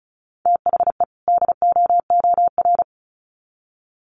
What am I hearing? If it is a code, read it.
Morse code, heard as a keyed tone, about 34 wpm, 700 Hz. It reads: THE DOOR
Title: Sets of 2 Words 35wpm